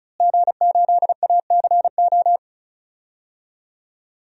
Morse code, heard as a keyed tone, about 36 wpm, 700 Hz. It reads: G8ACO
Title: DXCC-100 Call Signs - 5 Characters - Repeated 3x - 35wpm